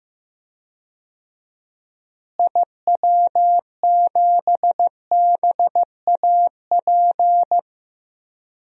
Morse code, heard as a keyed tone, about 15 wpm, 700 Hz. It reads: IW7BAP
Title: DXCC-100 Call Signs - 6 Characters - Repeated 3x - 15wpm